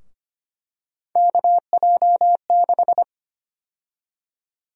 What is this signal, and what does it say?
Morse code, heard as a keyed tone, about 25 wpm, 700 Hz. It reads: KJ6
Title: USA Call Sign Prefixes - 25wpm